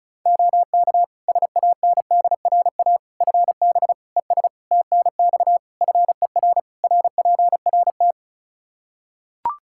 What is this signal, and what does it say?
Morse code, heard as a keyed tone, about 36 wpm, 700 Hz. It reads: OK SANDRA FB ES TNX FER RPRT
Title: Practice QSO - Exchange 2 - Info - 35wpm